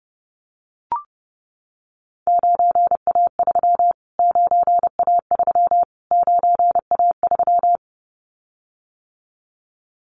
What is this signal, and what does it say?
Morse code, heard as a keyed tone, about 30 wpm, 700 Hz. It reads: 9A3 9A3 9A3
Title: Callsign Prefixes for Top 25 DXCC - Repeated 3x - 30wpm